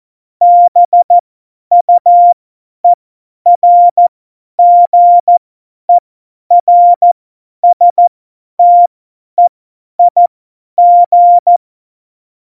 Morse code, heard as a keyed tone, about 14 wpm, spaced as for 10 wpm, 700 Hz. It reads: BUERGERSTEIG
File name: Morsekurs / Witze / Witz 08. Januar 2019, 50 F70